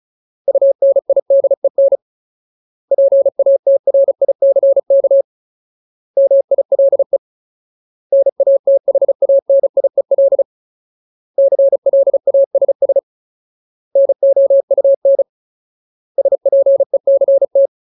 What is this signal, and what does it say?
Morse code, heard as a keyed tone, about 35 wpm, 550 Hz. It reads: UNIDEN PATRICK MILE NATHANIEL CLASS NOUN SPECT
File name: Ultimate Word List / 35WPM Ultimate